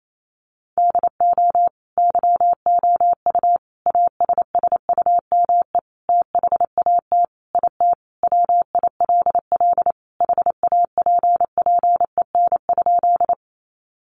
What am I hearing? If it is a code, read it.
Morse code, heard as a keyed tone, about 28 wpm, 700 Hz. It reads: DO YOU ASSUME THAT IT WILL HAPPEN?
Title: Sentences from Top 1500 Words 28wpm